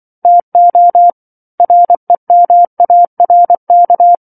Morse code, heard as a keyed tone, about 24 wpm, 700 Hz. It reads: TO REMARK